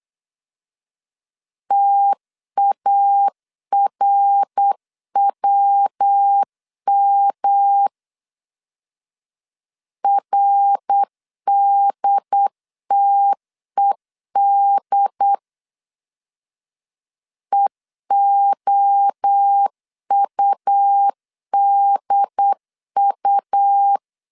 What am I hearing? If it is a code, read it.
Morse code, heard as a keyed tone, about 8 wpm, 800 Hz. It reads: TARWM RDTED EOUDU
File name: CW / DET / 1 3 1500C S B 8wpm 4141